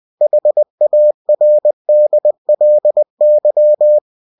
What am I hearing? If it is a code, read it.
Morse code, heard as a keyed tone, about 20 wpm, 600 Hz. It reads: HARDLY
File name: cw-books / winnie-the-pooh / winnie-20wpm-page-0063